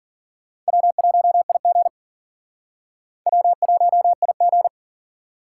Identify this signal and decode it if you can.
Morse code, heard as a keyed tone, about 51 wpm, 700 Hz. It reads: W1IG W1IG